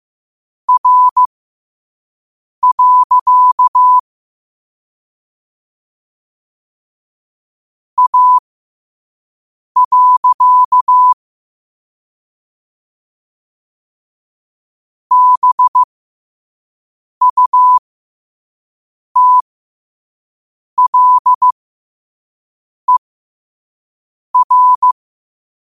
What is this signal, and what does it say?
Morse code, heard as a keyed tone, about 15 wpm, spaced as for 5 wpm, 1000 Hz. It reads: R. A. BUTLER
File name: qotdfiles / 5wpm